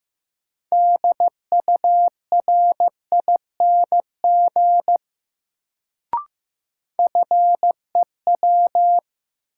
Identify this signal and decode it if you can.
Morse code, heard as a keyed tone, about 15 wpm, 700 Hz. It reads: DURING FEW
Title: Sets of 2 Words - New Word Drill - 15wpm